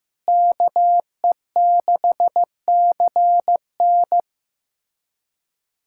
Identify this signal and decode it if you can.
Morse code, heard as a keyed tone, about 15 wpm, 700 Hz. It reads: KE6CN
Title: USA Callsign DE Callsign - 5 Characters - 15wpm